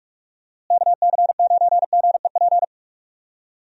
Morse code, heard as a keyed tone, about 46 wpm, 700 Hz. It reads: KC9GEP
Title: USA Call Signs - 6 Characters - Repeated 2x - 45wpm